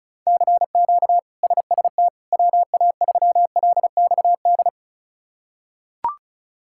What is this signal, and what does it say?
Morse code, heard as a keyed tone, about 36 wpm, 700 Hz. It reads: CQ SST WA3LXD
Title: K1USN SST QSO Practice - 35wpm